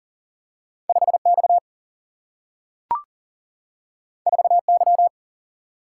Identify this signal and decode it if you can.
Morse code, heard as a keyed tone, about 38 wpm, 700 Hz. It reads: 5 X 4 Y
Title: Single-Letter Single-Number 40wpm